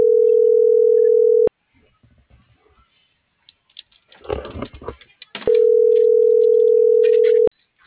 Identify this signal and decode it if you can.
Ringback tone (North America), 440 + 480 Hz.